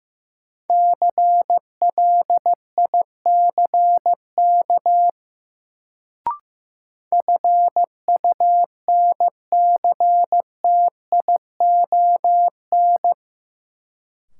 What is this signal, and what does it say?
Morse code, heard as a keyed tone, about 15 wpm, 700 Hz. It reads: CLICK FUNCTION
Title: Sentences from Top 1100 Words - New Word Drill - 15wpm